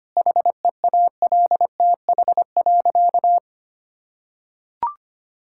Morse code, heard as a keyed tone, about 25 wpm, 700 Hz. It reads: HEALTH.
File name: practice / 25wpm / Sentences with 3-Word Phrases - 25wpm